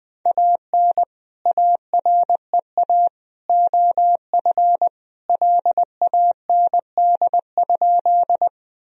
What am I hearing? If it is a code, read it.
Morse code, heard as a keyed tone, about 21 wpm, 700 Hz. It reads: AN AREA OF LAND?
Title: Sentences from Top 1000 Words - Review Part 2 - 20wpm